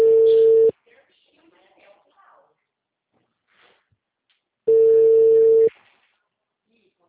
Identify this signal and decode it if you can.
Ringback tone (China, 450 Hz), with speech.
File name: out- -2022